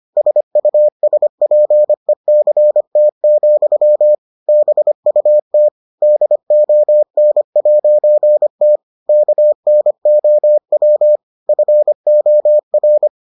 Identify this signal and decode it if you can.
Morse code, heard as a keyed tone, about 25 wpm, 600 Hz. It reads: SUSPECT, BUT DON'T KNOW FOR